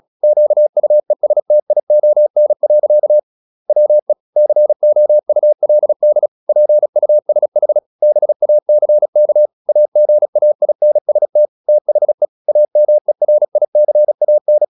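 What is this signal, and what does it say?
Morse code, heard as a keyed tone, about 36 wpm, 600 Hz. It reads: QUESTION. WE COULD PUSH BACK AGAINST THE AMERICAN
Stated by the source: Morse practice at 36 wpm, text of Ken Follett's novel Never, from CW med Gnister.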